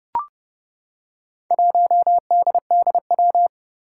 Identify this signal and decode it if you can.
Morse code, heard as a keyed tone, about 31 wpm, 700 Hz. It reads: 1DDW
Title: Four Challenging Letter-Number Combinations 30wpm